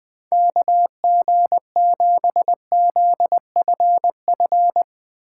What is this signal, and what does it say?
Morse code, heard as a keyed tone, about 22 wpm, 700 Hz. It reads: KG7ZFF
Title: Speed Racing - USA 2x3 Call Signs 20wpm